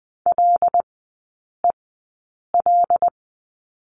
Morse code, heard as a keyed tone, about 20 wpm, spaced as for 8 wpm, 700 Hz. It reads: LEL
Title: 074 Lesson 13e - Letter L - Three Characters 20wpm